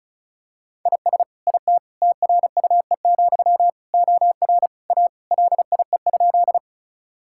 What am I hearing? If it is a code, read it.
Morse code, heard as a keyed tone, about 36 wpm, 700 Hz. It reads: IS IT TRUE, OR A LIE?
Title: Sentences from Top 1600 Words 35wpm